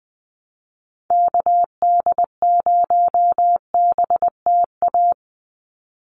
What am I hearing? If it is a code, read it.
Morse code, heard as a keyed tone, about 20 wpm, 700 Hz. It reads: KD0BTA
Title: USA Call Signs 20wpm